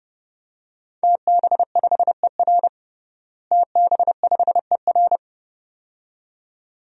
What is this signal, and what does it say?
Morse code, heard as a keyed tone, about 30 wpm, 700 Hz. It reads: TB5ER TB5ER
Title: DXCC-100 Call Signs - 5 Characters - Repeated 2x - 30wpm